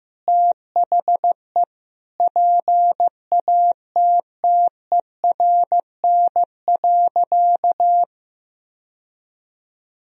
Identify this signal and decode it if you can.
Morse code, heard as a keyed tone, about 16 wpm, 700 Hz. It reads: THE PATTERN.
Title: Sentences from Top 1000 Words - Review Part 2 - 15wpm